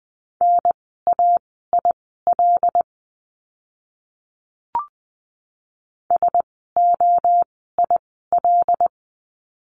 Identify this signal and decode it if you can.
Morse code, heard as a keyed tone, about 20 wpm, spaced as for 15 wpm, 700 Hz. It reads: NAIL SOIL
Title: 075 Lesson 13f - Letter L - Words 20wpm